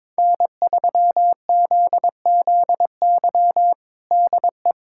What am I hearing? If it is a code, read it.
Morse code, heard as a keyed tone, about 21 wpm, 700 Hz. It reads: N3ZZY DE
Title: USA Callsign DE Callsign - 5 Characters - 22wpm